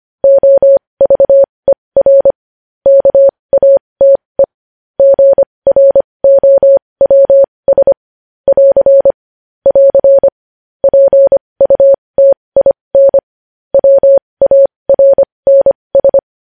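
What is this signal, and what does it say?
Morse code, heard as a keyed tone, about 25 wpm, spaced as for 20 wpm, 550 Hz. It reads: OVER KATE GROWS <AR> <AR> PUTIN WARNS
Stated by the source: Morse code news headlines.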